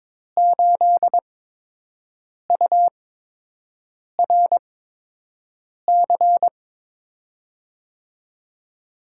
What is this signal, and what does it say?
Morse code, heard as a keyed tone, about 23 wpm, 700 Hz. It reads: 8 U R C
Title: USA Call Signs - 6 Characters - 8x Character Spacing - 22wpm